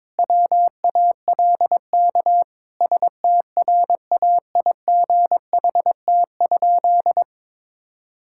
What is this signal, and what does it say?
Morse code, heard as a keyed tone, about 22 wpm, 700 Hz. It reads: WALK STRAIGHT?
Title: Sentences from Top 1000 Words - Review Part 6 - 22wpm